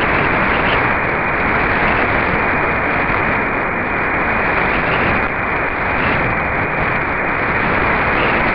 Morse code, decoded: none